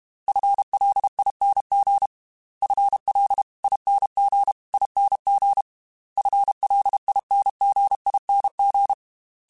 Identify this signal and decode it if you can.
Morse code, heard as a keyed tone, about 32 wpm, 800 Hz. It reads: FLING FLINGING FLINGING
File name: FL / FL mp3s / FL Prefixes 3x2 sp @30